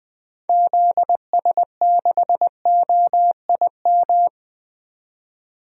Morse code, heard as a keyed tone, about 20 wpm, 700 Hz. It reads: ZS6OIM
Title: DXCC-100 Call Signs - 6 Characters - 20wpm